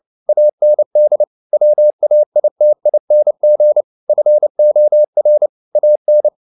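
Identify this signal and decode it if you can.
Morse code, heard as a keyed tone, about 29 wpm, 600 Hz. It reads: AND WAITING FOR AN
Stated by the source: Morse code at 29 wpm, sending Ken Follett's novel Never in English, from CW med Gnister.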